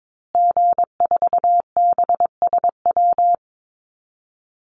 Morse code, heard as a keyed tone, about 22 wpm, 700 Hz. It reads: G4BSW
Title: Call Signs Repeated 3x - 22wpm